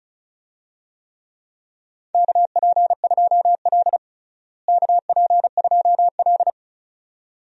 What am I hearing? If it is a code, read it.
Morse code, heard as a keyed tone, about 36 wpm, 700 Hz. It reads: KP2L KP2L
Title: DXCC-100 Call Signs - 4 Characters - Repeated 2x - 35wpm